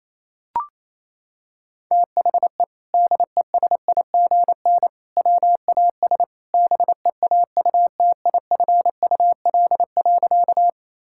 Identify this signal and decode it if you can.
Morse code, heard as a keyed tone, about 28 wpm, 700 Hz. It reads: THE DESIGN WAS BEAUTIFUL.
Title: Sentences from Top 1000 Words - Review Part 5 - 28wpm